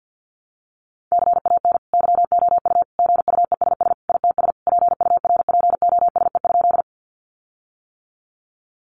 Morse code, heard as a keyed tone, about 50 wpm, 700 Hz. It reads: CAN YOU GUESS ITS PURPOSE?
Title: Sentences from Top 1500 Words 50wpm